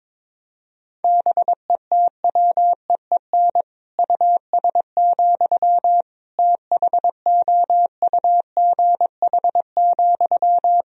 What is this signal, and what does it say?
Morse code, heard as a keyed tone, about 22 wpm, 700 Hz. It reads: BETWEEN US, THOUGH,
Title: Sentences from Top 1000 Words - Review Part 2 - 22wpm